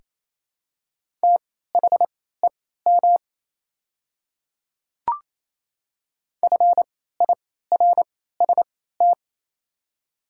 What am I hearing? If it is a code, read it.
Morse code, heard as a keyed tone, about 28 wpm, spaced as for 16 wpm, 700 Hz. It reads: THEM FIRST
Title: 3x Character Spacing - Top 100 Words - 28wpm